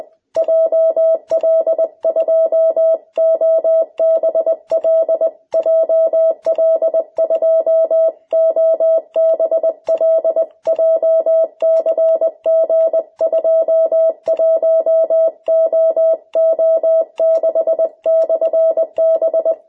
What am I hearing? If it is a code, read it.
Morse code, heard as a keyed tone, about 20 wpm, 660 Hz. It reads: JL2OBL JL2OBL JCG21OO6/B
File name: jh8cbh / 2021cw